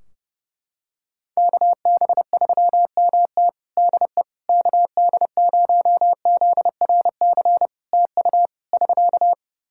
Morse code, heard as a keyed tone, about 30 wpm, 700 Hz. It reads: KB3MT DE KD0ZRC TU <SK>
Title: Practice QSO - Exchange 4 - Ending - 30wpm